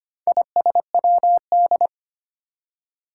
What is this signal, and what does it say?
Morse code, heard as a keyed tone, about 25 wpm, 700 Hz. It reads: ISWD
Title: Four Challenging Letter-Number Combinations 25wpm